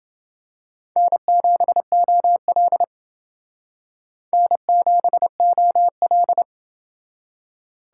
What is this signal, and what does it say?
Morse code, heard as a keyed tone, about 28 wpm, 700 Hz. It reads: N7OL N7OL